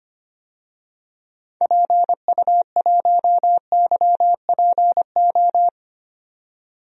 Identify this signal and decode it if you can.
Morse code, heard as a keyed tone, about 25 wpm, 700 Hz. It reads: PU1YPO
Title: DXCC-100 Call Signs - 6 Characters - 25wpm